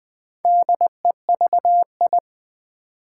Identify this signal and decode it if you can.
Morse code, heard as a keyed tone, about 20 wpm, 690 Hz. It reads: DEVI
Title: Build A Word - 2 Letters At A Time - 6-Letter Words - 20wpm